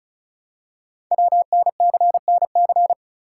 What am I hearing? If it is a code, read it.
Morse code, heard as a keyed tone, about 36 wpm, 700 Hz. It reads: WNCNC
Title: Code Groups - 5 Letters - 35wpm